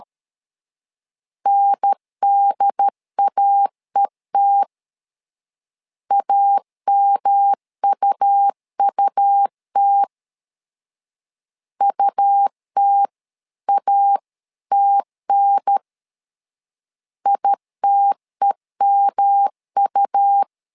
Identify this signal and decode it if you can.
Morse code, heard as a keyed tone, about 13 wpm, 800 Hz. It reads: NDAET AMUUT UTATN ITEMU
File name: LEZIONI / 3 MWRO / 3 M 12wpm 1541